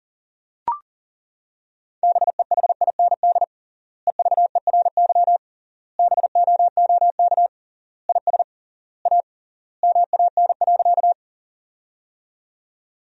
Morse code, heard as a keyed tone, about 40 wpm, 700 Hz. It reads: BEHIND EVERY BOOK IS A MAN.
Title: 3x Word-Spacing - Sentences from Top 500 Words - 40wpm